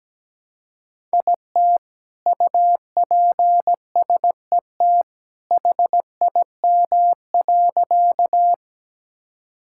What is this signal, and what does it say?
Morse code, heard as a keyed tone, about 17 wpm, 700 Hz. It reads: IT UPSET HIM.
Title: Sentences from Top 1000 Words - Review Part 5 - 17wpm